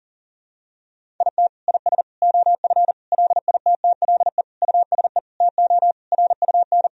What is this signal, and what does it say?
Morse code, heard as a keyed tone, about 40 wpm, 700 Hz. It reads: IT IS OF LITTLE USE TO RUN
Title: Sentences from Top 1000 Words - Review Part 1 - 40wpm